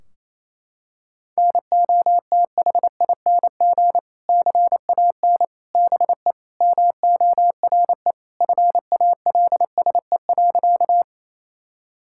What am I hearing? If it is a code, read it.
Morse code, heard as a keyed tone, about 28 wpm, 700 Hz. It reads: NOTHING CAN BE MORE FALSE.
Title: Sentences from Top 1600 Words 28wpm